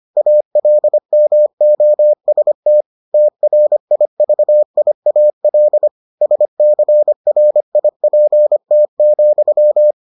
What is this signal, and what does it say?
Morse code, heard as a keyed tone, about 25 wpm, 600 Hz. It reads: ALMOST TRIVIAL SCRIPT,